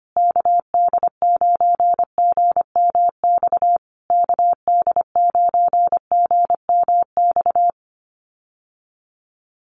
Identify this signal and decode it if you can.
Morse code, heard as a keyed tone, about 25 wpm, 700 Hz. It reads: KD9GMX KD9GMX
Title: USA Call Signs Repeated 3x 25wpm